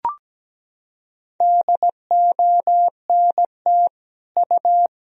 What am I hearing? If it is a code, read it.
Morse code, heard as a keyed tone, about 17 wpm, 700 Hz. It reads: DONT U
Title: Sentences from Top 300 Words and Abbreviations 17wpm